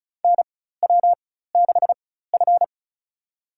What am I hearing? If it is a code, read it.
Morse code, heard as a keyed tone, about 36 wpm, spaced as for 16 wpm, 700 Hz. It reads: NW6F
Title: USA Call Signs - 4 Characters - 4x Character Spacing - 35wpm